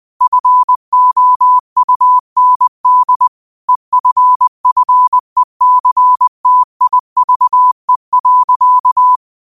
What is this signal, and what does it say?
Morse code, heard as a keyed tone, about 20 wpm, 1000 Hz. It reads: FOUND EFFECTIVE.
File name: 20wpm